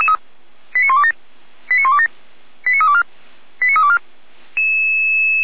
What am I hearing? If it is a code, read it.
Five-tone selective call (ZVEI), heard as quick runs of five tones, tones of about 70 ms. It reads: 89127 89127 89324 89324
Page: /455/SDR/ZVEI/89330